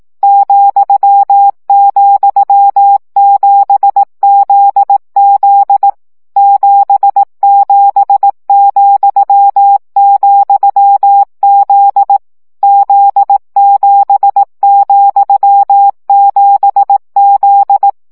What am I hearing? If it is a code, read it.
Morse code, heard as a keyed tone, about 18 wpm, 800 Hz. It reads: ,,7ZZ 77,,Z Z7,7Z
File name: CW gyakorlas z 7 vesszo 15 baud